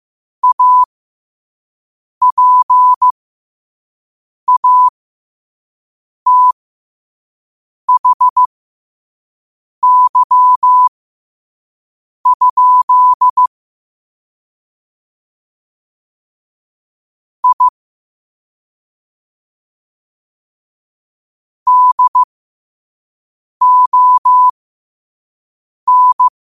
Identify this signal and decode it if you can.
Morse code, heard as a keyed tone, about 15 wpm, spaced as for 5 wpm, 1000 Hz. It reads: APATHY? I DON
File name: qotdfiles / 5wpm